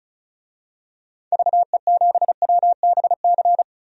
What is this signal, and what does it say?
Morse code, heard as a keyed tone, about 34 wpm, 700 Hz. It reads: VE7WBC
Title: DXCC-100 Call Signs - 6 Characters - 35wpm